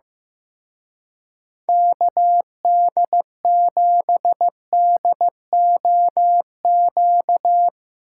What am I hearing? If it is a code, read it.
Morse code, heard as a keyed tone, about 18 wpm, 700 Hz. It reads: KD7DOQ
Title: Speed Racing - USA 2x3 Call Signs 15wpm